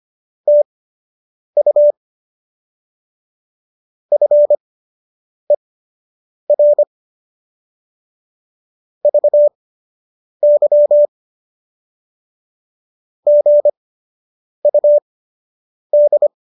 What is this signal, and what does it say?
Morse code, heard as a keyed tone, about 25 wpm, spaced as for 8 wpm, 590 Hz. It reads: TU FER VY GUD
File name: qso006 08